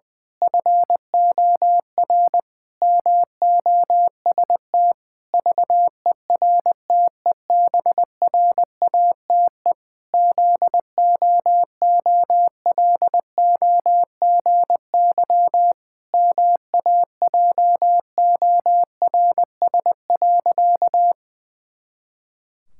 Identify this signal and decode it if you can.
Morse code, heard as a keyed tone, about 20 wpm, 700 Hz. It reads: FOR MOST VERTEBRATE ZOOLOGY MAJORS.
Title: Pangram Challenge v2 - 20wpm